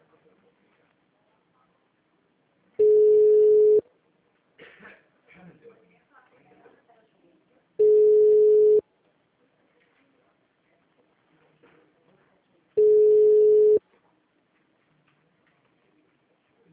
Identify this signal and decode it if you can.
Ringback tone (Europe, 425 Hz), with speech.